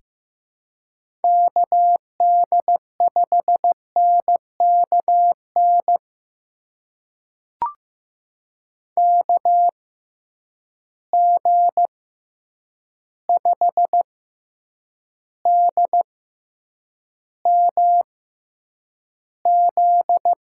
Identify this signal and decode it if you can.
Morse code, heard as a keyed tone, about 15 wpm, spaced as for 8 wpm, 700 Hz. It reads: KD5NKN KG5DMZ
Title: USA Call Signs - 6 Characters - 6x Character Spacing - 15wpm